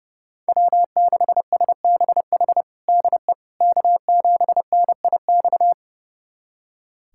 Morse code, heard as a keyed tone, about 30 wpm, 700 Hz. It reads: W6SBH DE K7NIX
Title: USA Callsign DE Callsign 30wpm